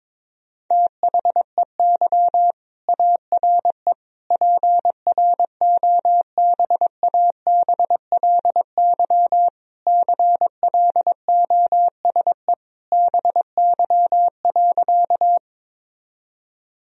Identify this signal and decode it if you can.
Morse code, heard as a keyed tone, about 22 wpm, 700 Hz. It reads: THEY ARE PROBABLY CLOSE BY.
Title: Sentences from Top 1000 Words - Review Part 4 - 22wpm